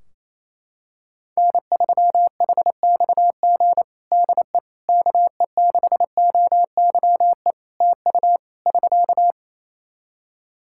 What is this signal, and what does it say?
Morse code, heard as a keyed tone, about 27 wpm, 700 Hz. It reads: N3HXG DE KE6OYE TU <SK>
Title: Practice QSO - Exchange 4 - Ending - 28wpm